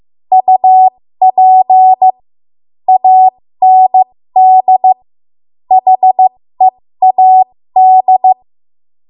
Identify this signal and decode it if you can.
Morse code, heard as a keyed tone, about 15 wpm, spaced as for 13 wpm, 750 Hz. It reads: UP AND HEAD
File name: Archive / 13 WPM / 13WPM